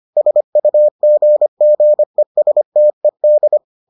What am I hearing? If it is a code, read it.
Morse code, heard as a keyed tone, about 25 wpm, 600 Hz. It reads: SUGGESTED